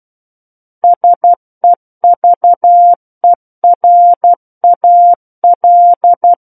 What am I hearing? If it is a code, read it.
Morse code, heard as a keyed tone, about 12 wpm, 700 Hz. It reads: SEVERAL
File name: Book One / 03-Chapter 3 - 12 WPM